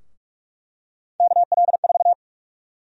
Morse code, heard as a keyed tone, about 46 wpm, 700 Hz. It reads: KL4
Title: USA Call Sign Prefixes - Repeated 2x - 45wpm